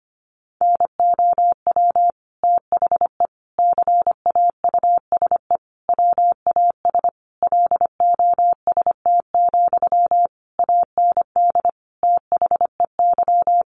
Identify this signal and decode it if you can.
Morse code, heard as a keyed tone, about 25 wpm, 700 Hz. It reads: NOW THE CAUSE WAS LOST, AND THEY
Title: Sentences from Top 1500 Words 25wpm